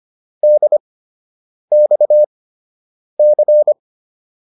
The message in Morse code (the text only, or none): DXC